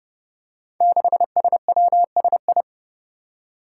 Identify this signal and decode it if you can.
Morse code, heard as a keyed tone, about 30 wpm, 700 Hz. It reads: 6SWSI
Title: Five Challenging Letter-Number Combinations 30wpm